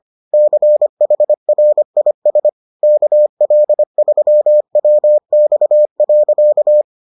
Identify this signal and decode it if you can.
Morse code, heard as a keyed tone, about 25 wpm, 600 Hz. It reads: CHRIS KL3WX.